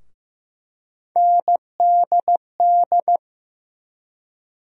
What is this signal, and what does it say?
Morse code, heard as a keyed tone, about 15 wpm, 700 Hz. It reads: NDD
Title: USA Call Sign Postfixes - 3 Letters 15wpm